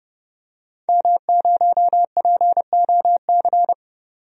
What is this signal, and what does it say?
Morse code, heard as a keyed tone, about 30 wpm, 700 Hz. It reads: M0POC
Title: DXCC-100 Call Signs - 5 Characters - 30wpm